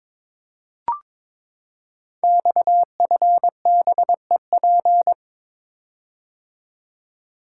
Morse code, heard as a keyed tone, about 23 wpm, 700 Hz. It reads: XFBEP
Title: Code Groups - 5 Letter:Numbers - 22wpm